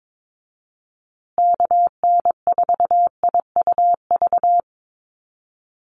Morse code, heard as a keyed tone, about 22 wpm, 700 Hz. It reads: KN4IUV
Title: USA Call Signs Repeated 3x 22wpm